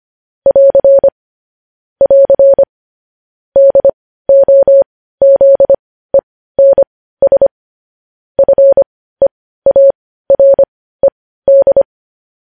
Morse code, heard as a keyed tone, about 25 wpm, spaced as for 15 wpm, 550 Hz. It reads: <AR> <AR> DOZENS FEARED
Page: news-headlines-in-morse-code-at-15-wpm